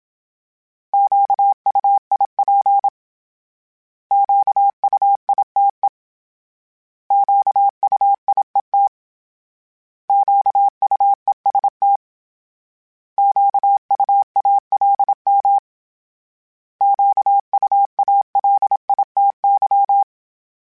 Morse code, heard as a keyed tone, about 27 wpm, 800 Hz. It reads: QUIP QUITE QUIET QUEST QUALM QUALITY
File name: Misc / Misc-mp3s / Misc @25